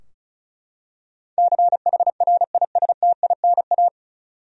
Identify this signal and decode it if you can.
Morse code, heard as a keyed tone, about 36 wpm, 700 Hz. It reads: CHRISTINA
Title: USA Common Names - 35wpm